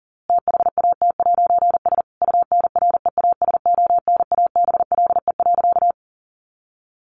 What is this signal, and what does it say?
Morse code, heard as a keyed tone, about 40 wpm, 700 Hz. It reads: THAT'S UNREASONABLE.
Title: A Sound Of Thunder 40wpm